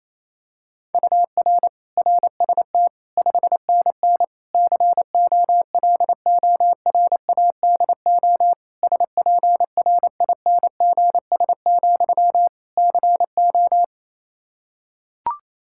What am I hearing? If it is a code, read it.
Morse code, heard as a keyed tone, about 28 wpm, 700 Hz. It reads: UR RST 5NN COLORADO SPRINGS, CO
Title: Practice QSO - Exchange 1 - Beginning - 28wpm